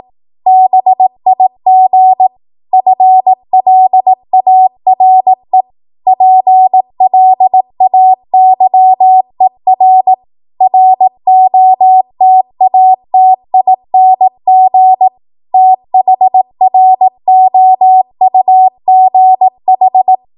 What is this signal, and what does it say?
Morse code, heard as a keyed tone, about 18 wpm, 750 Hz. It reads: BIG FLARE PLAYER ROTATING THROUGH
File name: Archive / 18 WPM / 18WPM